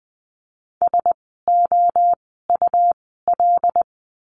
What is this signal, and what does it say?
Morse code, heard as a keyed tone, about 20 wpm, spaced as for 14 wpm, 700 Hz. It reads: SOUL
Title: 101 Lesson 17f - Letter u - Words 20wpm